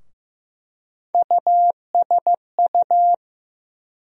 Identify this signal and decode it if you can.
Morse code, heard as a keyed tone, about 15 wpm, 700 Hz. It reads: USU
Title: USA Call Sign Postfixes - 3 Letters 15wpm